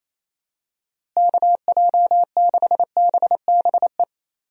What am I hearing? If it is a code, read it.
Morse code, heard as a keyed tone, about 28 wpm, 700 Hz. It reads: KJ6BBE
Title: USA Call Signs - 6 Characters - Repeated 3x - 28wpm